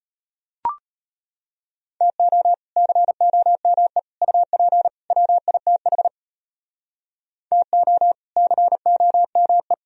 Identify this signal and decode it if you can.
Morse code, heard as a keyed tone, about 33 wpm, 700 Hz. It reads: TO COME UP WITH TO COME